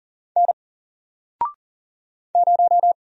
Morse code, heard as a keyed tone, about 40 wpm, 700 Hz. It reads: N 0
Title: Single Letter-Number 40wpm